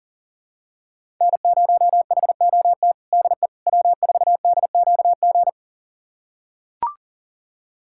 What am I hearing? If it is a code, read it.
Morse code, heard as a keyed tone, about 40 wpm, 700 Hz. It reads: N0HOT DE W4DQG
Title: USA Callsign DE Callsign - 5 Characters - 40wpm